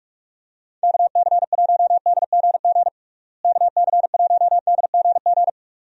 Morse code, heard as a keyed tone, about 44 wpm, 700 Hz. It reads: KC1DGG KC1DGG
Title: USA Call Signs - 6 Characters - Repeated 2x - 45wpm